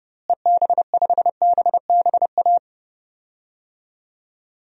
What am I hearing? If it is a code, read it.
Morse code, heard as a keyed tone, about 30 wpm, 700 Hz. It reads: EB5BBA
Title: DXCC-100 Call Signs - 6 Characters - Repeated 3x - 30wpm